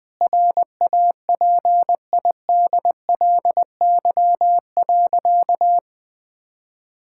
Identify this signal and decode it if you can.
Morse code, heard as a keyed tone, about 20 wpm, 700 Hz. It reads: RAPIDLY.
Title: Sentences with 5-Word Phrases - 20wpm